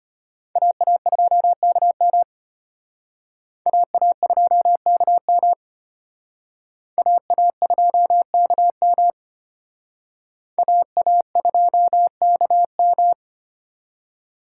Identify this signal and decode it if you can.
Morse code, heard as a keyed tone, about 33 wpm, 700 Hz. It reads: AA2KM AA2KM AA2KM AA2KM